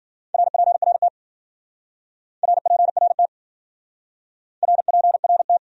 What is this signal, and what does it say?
Morse code, heard as a keyed tone, about 52 wpm, 700 Hz. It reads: RPRT RPRT RPRT